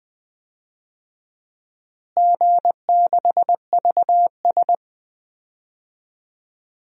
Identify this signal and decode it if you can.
Morse code, heard as a keyed tone, about 21 wpm, 700 Hz. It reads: G6VS
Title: DXCC-100 Call Signs - 4 Characters - Repeated 2x - 20wpm